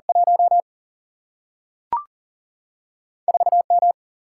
Morse code, heard as a keyed tone, about 40 wpm, 700 Hz. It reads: 1 4M
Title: Single-Letter Single-Number 40wpm